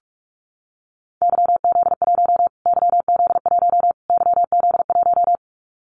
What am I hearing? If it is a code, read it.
Morse code, heard as a keyed tone, about 44 wpm, 700 Hz. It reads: YZ1 YZ1 YZ1
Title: Callsign Prefixes for Top 25 DXCC - Repeated 3x - 45wpm